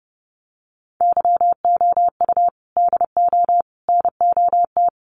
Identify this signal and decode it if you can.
Morse code, heard as a keyed tone, about 30 wpm, 700 Hz. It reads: YOU DO NOT
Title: Sets of 5 Words 30wpm